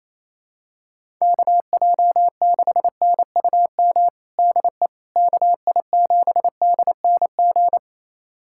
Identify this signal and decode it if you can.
Morse code, heard as a keyed tone, about 28 wpm, 700 Hz. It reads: KJ6NUM DE KI7DNG
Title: USA Callsign DE Callsign - 6 Characters - 28wpm